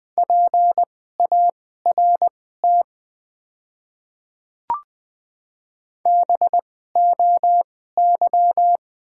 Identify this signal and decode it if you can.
Morse code, heard as a keyed tone, about 20 wpm, spaced as for 14 wpm, 700 Hz. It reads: PART BOY
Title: 2x Character Spacing - Top 300 Words - 20wpm